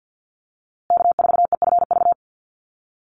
Morse code, heard as a keyed tone, about 50 wpm, 700 Hz. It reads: K4EFV
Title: USA Call Signs Repeated 3x 50wpm